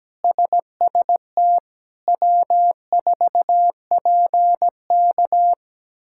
Morse code, heard as a keyed tone, about 17 wpm, 700 Hz. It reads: SST W4PK
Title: K1USN SST QSO Practice - 17wpm